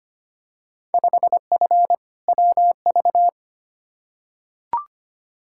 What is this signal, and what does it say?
Morse code, heard as a keyed tone, about 25 wpm, 700 Hz. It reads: 5F WV
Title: Field Day Exchanges Encore - 25wpm